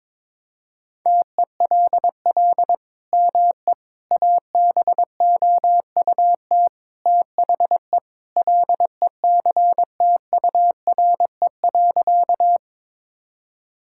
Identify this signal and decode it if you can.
Morse code, heard as a keyed tone, about 22 wpm, 700 Hz. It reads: TELL ME ABOUT THE LECTURE.
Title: Sentences from Top 1600 Words 22wpm